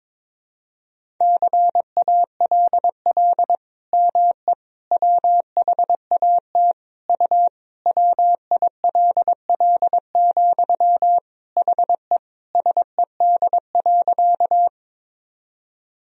Morse code, heard as a keyed tone, about 22 wpm, 700 Hz. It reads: CALL ME WHAT U WILL, HE SED.
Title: Sentences from Top 200 Words and Abbreviations 22wpm